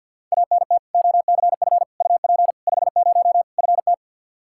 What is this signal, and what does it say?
Morse code, heard as a keyed tone, about 46 wpm, 700 Hz. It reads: ANT OCF UP 50 FT
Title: Practice QSO - Exchange 2 - Info - 50wpm